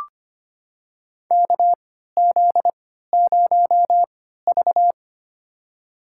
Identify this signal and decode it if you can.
Morse code, heard as a keyed tone, about 25 wpm, spaced as for 14 wpm, 700 Hz. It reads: KZ0V
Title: USA Call Signs - 4 Characters - 3x Character Spacing - 25wpm